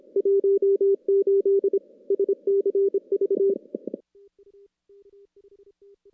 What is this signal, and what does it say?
Morse code, heard as a keyed tone, about 25 wpm, 400 Hz. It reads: 18 SCV
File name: contests / CW-SS-k5tr- -0000